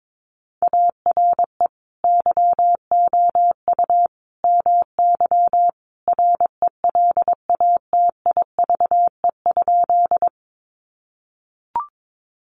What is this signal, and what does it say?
Morse code, heard as a keyed tone, about 22 wpm, 700 Hz. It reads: ARE YOU MY RELATIVE?
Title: Sentences from Top 1400 Words 22wpm